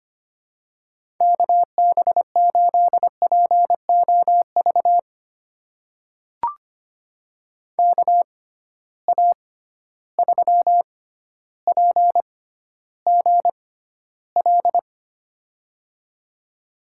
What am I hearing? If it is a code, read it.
Morse code, heard as a keyed tone, about 25 wpm, spaced as for 13 wpm, 700 Hz. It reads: KB8POV KA3PGL
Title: USA Call Signs - 6 Characters - 6x Character Spacing - 25wpm